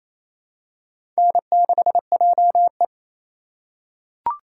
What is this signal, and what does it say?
Morse code, heard as a keyed tone, about 27 wpm, 700 Hz. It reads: N6JE
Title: USA Call Signs - 2x Character Spacing - 28wpm